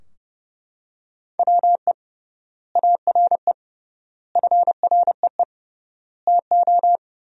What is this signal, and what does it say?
Morse code, heard as a keyed tone, about 30 wpm, 700 Hz. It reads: WE ARE FREE TO
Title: 3x Word-Spacing - Sentences from Top 500 Words - 30wpm